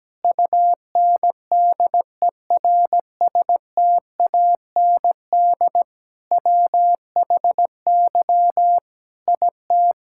Morse code, heard as a keyed tone, about 17 wpm, 700 Hz. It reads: UNDERSTAND WHY IT
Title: Sentences from Top 1000 Words - Review Part 5 - 17wpm